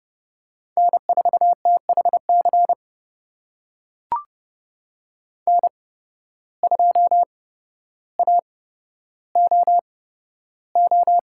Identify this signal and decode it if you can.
Morse code, heard as a keyed tone, about 30 wpm, spaced as for 13 wpm, 700 Hz. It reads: N4THC N2AOO
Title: USA Call Signs - 5 Characters - 8x Character Spacing - 30wpm